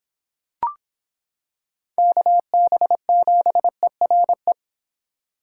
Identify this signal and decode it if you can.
Morse code, heard as a keyed tone, about 26 wpm, 700 Hz. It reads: KB7ERE